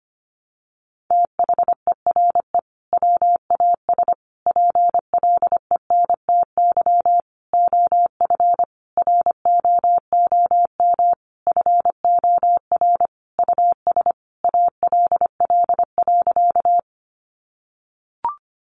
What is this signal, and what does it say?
Morse code, heard as a keyed tone, about 25 wpm, 700 Hz. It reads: THERE WAS PLENTY OF ROOM FOR US ALL.
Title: Sentences from Top 1300 Words 25wpm